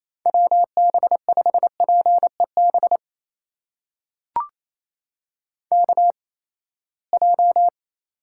Morse code, heard as a keyed tone, about 28 wpm, spaced as for 19 wpm, 700 Hz. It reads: WB5PEB KJ
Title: USA Call Signs - 8x Character Spacing - 28wpm